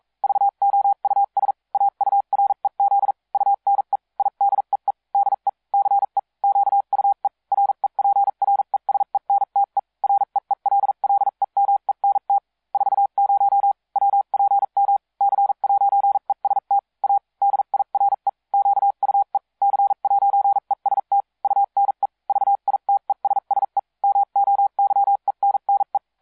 Morse code, heard as a keyed tone, about 42 wpm, spaced as for 36 wpm, 800 Hz. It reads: VOUS AUREZ UNE IDEE DE CE QUE REPRESENTE REELLEMENT 40 WPM C'EST A DIRE QUE C'EST UNE VITESSE MOYENNE